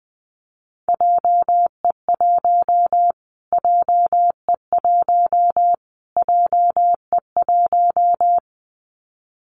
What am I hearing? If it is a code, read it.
Morse code, heard as a keyed tone, about 20 wpm, 700 Hz. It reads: JE1 JE1 JE1
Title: Callsign Prefixes for Top 25 DXCC - Repeated 3x - 20wpm